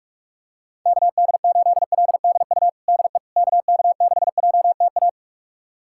Morse code, heard as a keyed tone, about 41 wpm, 700 Hz. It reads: KD8LDU DE KK6JTA
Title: USA Callsign DE Callsign - 6 Characters - 45wpm